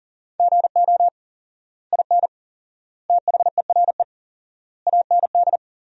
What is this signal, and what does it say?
Morse code, heard as a keyed tone, about 40 wpm, 700 Hz. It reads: GO IN THERE AND
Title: 4x Word-Spacing - Sentences from Top 500 Words - 40wpm